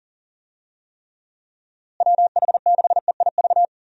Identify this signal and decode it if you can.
Morse code, heard as a keyed tone, about 40 wpm, 700 Hz. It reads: WH6EIV
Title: USA Call Signs - 6 Characters - Repeated 3x - 40wpm